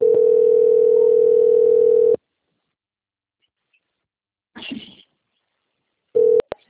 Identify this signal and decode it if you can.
Ringback tone (North America), 440 + 480 Hz.